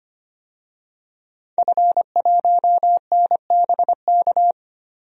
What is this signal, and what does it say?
Morse code, heard as a keyed tone, about 25 wpm, 700 Hz. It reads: F1NBK
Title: DXCC-100 Call Signs - 5 Characters - Repeated 2x - 25wpm